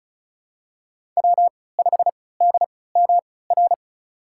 Morse code, heard as a keyed tone, about 36 wpm, spaced as for 20 wpm, 700 Hz. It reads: W5DMR
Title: USA Call Signs - 5 Characters - 3x Character Spacing - 35wpm